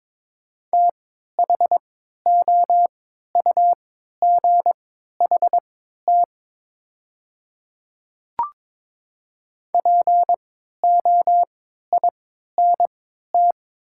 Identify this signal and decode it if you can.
Morse code, heard as a keyed tone, about 22 wpm, spaced as for 12 wpm, 700 Hz. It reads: THOUGHT POINT
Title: 3x Character Spacing - Top 300 Words - 22wpm